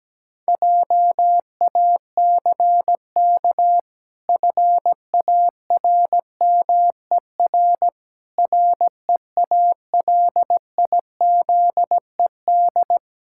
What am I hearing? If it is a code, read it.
Morse code, heard as a keyed tone, about 17 wpm, 700 Hz. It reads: JACK FARMER REALIZED
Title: Pangram Challenge v2 - 17wpm